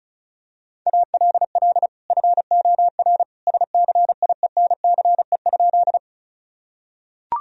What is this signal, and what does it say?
Morse code, heard as a keyed tone, about 36 wpm, 700 Hz. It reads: ALL FOR SCIENCE?
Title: Sentences from Top 1000 Words - Review Part 2 - 35wpm